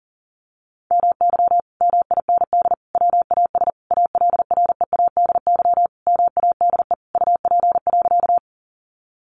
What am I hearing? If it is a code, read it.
Morse code, heard as a keyed tone, about 40 wpm, 700 Hz. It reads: MY MIND WAS ALREADY MADE UP.
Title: Sentences from Top 700 Words 40wpm